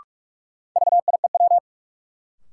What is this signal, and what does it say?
Morse code, heard as a keyed tone, about 47 wpm, 700 Hz. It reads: VIEW
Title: Sets of 5 Words - New Word Drill - 45wpm